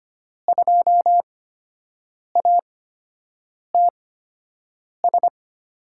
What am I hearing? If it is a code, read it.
Morse code, heard as a keyed tone, about 25 wpm, spaced as for 7 wpm, 700 Hz. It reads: 2ATS
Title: USA Call Signs - 5 Characters - 8x Character Spacing - 25wpm